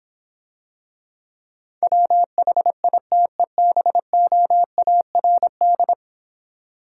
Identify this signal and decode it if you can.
Morse code, heard as a keyed tone, about 26 wpm, 700 Hz. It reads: WHITEBOARD